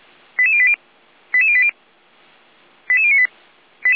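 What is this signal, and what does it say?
Five-tone selective call (ZVEI), heard as quick runs of five tones, tones of about 70 ms. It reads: 88080 88080 88098
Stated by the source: ZVEI five-tone call 88081 received on an SDR.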